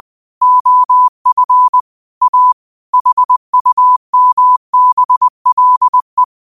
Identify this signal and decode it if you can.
Morse code, heard as a keyed tone, about 20 wpm, 1000 Hz. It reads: OF A HUMBLE
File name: qotdfiles / 20wpm